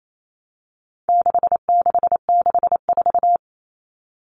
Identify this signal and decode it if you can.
Morse code, heard as a keyed tone, about 28 wpm, 700 Hz. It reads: 6664
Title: Numbers - 4-Digit 28wpm